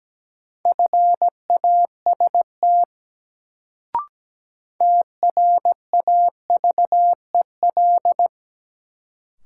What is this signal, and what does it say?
Morse code, heard as a keyed tone, about 17 wpm, 700 Hz. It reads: FAST TRAVEL
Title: Sentences from Top 400 Words - New Word Drill - 17wpm